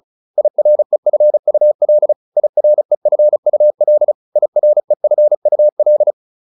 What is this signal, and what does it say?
Morse code, heard as a keyed tone, about 34 wpm, 600 Hz. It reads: IREFUL IREFUL IREFUL